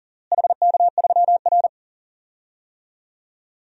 Morse code, heard as a keyed tone, about 37 wpm, 700 Hz. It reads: HK3R
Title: DXCC-100 Call Signs - 4 Characters - Repeated 3x - 40wpm